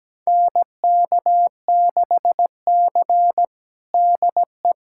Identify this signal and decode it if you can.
Morse code, heard as a keyed tone, about 17 wpm, 700 Hz. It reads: NK6C DE
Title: USA Callsign DE Callsign - 4 Characters - 17wpm